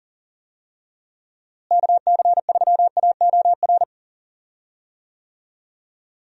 Morse code, heard as a keyed tone, about 40 wpm, 700 Hz. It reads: KC3AOR
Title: USA Call Signs - 6 Characters - Repeated 2x - 40wpm